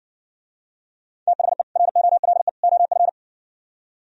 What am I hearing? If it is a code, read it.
Morse code, heard as a keyed tone, about 60 wpm, 700 Hz. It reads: THE ROLE OF